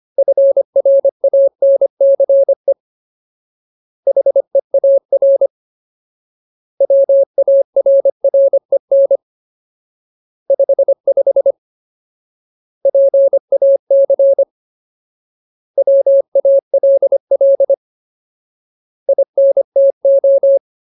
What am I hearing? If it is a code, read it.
Morse code, heard as a keyed tone, about 25 wpm, 550 Hz. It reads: FRANCE HEAR WARREN 55 PAC WALL INTO
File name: Ultimate Word List / 25WPM Ultimate